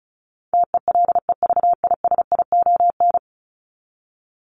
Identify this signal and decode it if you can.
Morse code, heard as a keyed tone, about 34 wpm, 700 Hz. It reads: TELEVISION
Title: Top 501-750 Words 35wpm